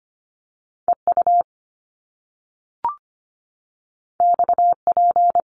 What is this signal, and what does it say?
Morse code, heard as a keyed tone, about 25 wpm, 700 Hz. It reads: E U X P
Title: Top Two Letter Combinations in Words 25wpm